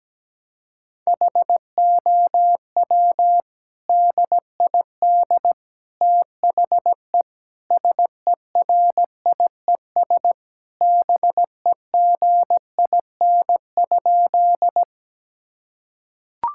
Morse code, heard as a keyed tone, about 17 wpm, 700 Hz. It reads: HOW DID THE SERIES BEGIN?
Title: Sentences from Top 1000 Words - Review Part 5 - 17wpm